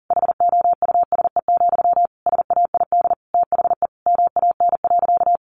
Morse code, heard as a keyed tone, about 40 wpm, 700 Hz. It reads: HOUSE, SAID THE MAN.
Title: Sentences from Top 400 Words 40wpm